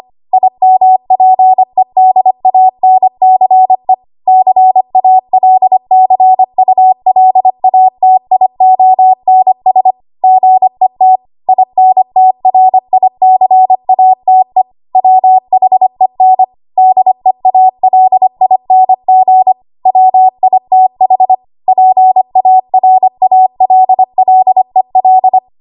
Morse code, heard as a keyed tone, about 25 wpm, 750 Hz. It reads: IMPEDANCE CALCULATIONS GET INTRICATE WHEN DEALING WITH PARALLEL
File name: Archive / 25 WPM / 25WPM